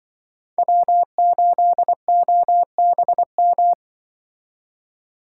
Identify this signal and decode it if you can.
Morse code, heard as a keyed tone, about 23 wpm, 700 Hz. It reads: W8OBM
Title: Speed Racing - USA 1x3 Call Signs 20wpm